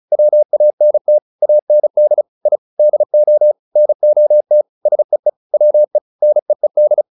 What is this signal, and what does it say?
Morse code, heard as a keyed tone, about 34 wpm, 600 Hz. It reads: WANT AND I DO NOT SEE WE NEED